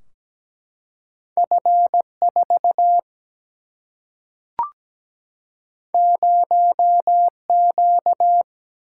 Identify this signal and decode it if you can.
Morse code, heard as a keyed tone, about 17 wpm, 700 Hz. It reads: F 4 0 Q
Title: Single-Letter Single-Number 17wpm